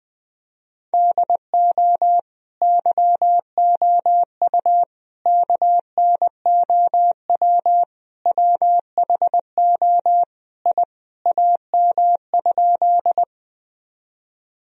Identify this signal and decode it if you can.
Morse code, heard as a keyed tone, about 20 wpm, 700 Hz. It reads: DO YOU KNOW WHO I AM?
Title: Sentences from Top 1000 Words - Review Part 2 - 20wpm